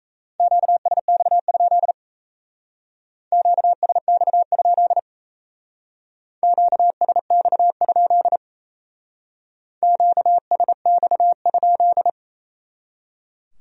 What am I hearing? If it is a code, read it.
Morse code, heard as a keyed tone, about 35 wpm, 700 Hz. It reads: QSX? QSX? QSX? QSX?